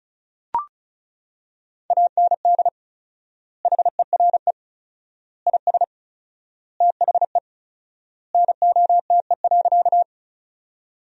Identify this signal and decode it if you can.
Morse code, heard as a keyed tone, about 34 wpm, 700 Hz. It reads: AND HERE IS THE NOTE.
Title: 4x Word-Spacing - Sentences from Top 500 Words - 35wpm